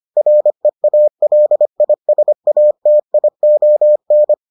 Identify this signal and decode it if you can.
Morse code, heard as a keyed tone, about 25 wpm, 600 Hz. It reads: REALISATION